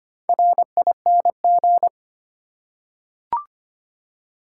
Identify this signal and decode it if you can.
Morse code, heard as a keyed tone, about 25 wpm, 700 Hz. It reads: RING
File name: practice / 25wpm / Easy Words with Hints - 25wpm